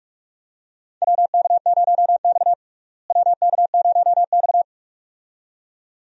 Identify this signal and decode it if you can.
Morse code, heard as a keyed tone, about 46 wpm, 700 Hz. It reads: WK0X WK0X
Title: USA Call Signs - 4 Characters - Repeated 2x - 45wpm